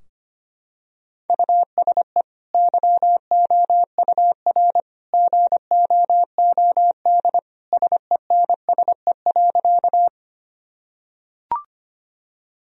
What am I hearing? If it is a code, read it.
Morse code, heard as a keyed tone, about 25 wpm, 700 Hz. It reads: USE YOUR GOOD SENSE.
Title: Sentences from Top 1000 Words - Review Part 4 - 25wpm